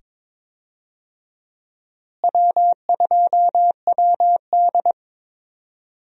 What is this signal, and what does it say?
Morse code, heard as a keyed tone, about 22 wpm, 700 Hz. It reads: W2WD
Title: USA Call Signs - 4 Characters - Repeated 2x - 22wpm